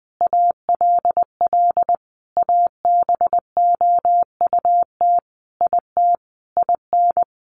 Morse code, heard as a keyed tone, about 20 wpm, 700 Hz. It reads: ALL ABOUT IT IN
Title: Sentences from Top 900 Words 20wpm